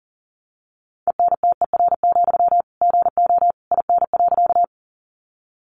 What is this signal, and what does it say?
Morse code, heard as a keyed tone, about 40 wpm, 700 Hz. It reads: ENTER, GO IN.
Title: Sentences from Top 1300 Words 40wpm